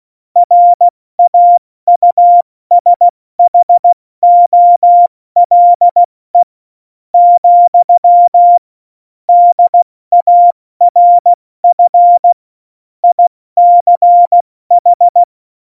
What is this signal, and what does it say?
Morse code, heard as a keyed tone, about 16 wpm, spaced as for 13 wpm, 700 Hz. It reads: RAUSHOLE, DARF ICH
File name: Morsekurs / Witze / Witz 18. September 2018, 70 F80